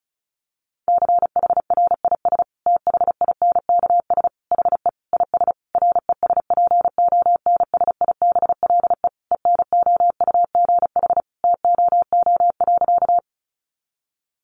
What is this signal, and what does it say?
Morse code, heard as a keyed tone, about 34 wpm, 700 Hz. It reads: CHRIS THINKS HE IS RESPONSIBLE ENOUGH TOO.
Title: Additional Easy Phrases 35wpm